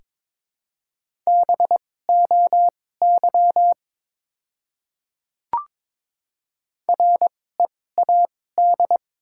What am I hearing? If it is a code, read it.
Morse code, heard as a keyed tone, about 23 wpm, spaced as for 16 wpm, 700 Hz. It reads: BOY READ
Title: 2x Character Spacing - Top 500 Words - 22wpm